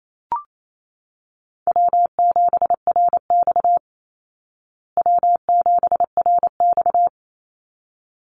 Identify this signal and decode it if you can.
Morse code, heard as a keyed tone, about 28 wpm, 700 Hz. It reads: W7RX W7RX
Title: Easy Call Signs Repeated 2x - 28wpm